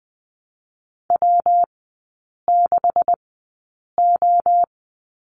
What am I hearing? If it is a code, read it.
Morse code, heard as a keyed tone, about 21 wpm, 700 Hz. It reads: W 6 O
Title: 172 Lesson 28e - Number 9 - Three Characters 20wpm